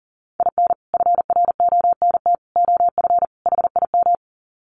Morse code, heard as a keyed tone, about 40 wpm, 700 Hz. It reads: IN FRONT OF HIM
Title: Sets of 4 Words 40wpm